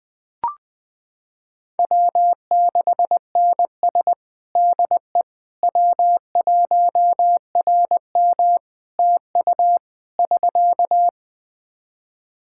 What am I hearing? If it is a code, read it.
Morse code, heard as a keyed tone, about 20 wpm, 700 Hz. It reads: W6NS DE W1RM TU <SK>
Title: K1USN SST QSO Practice 20wpm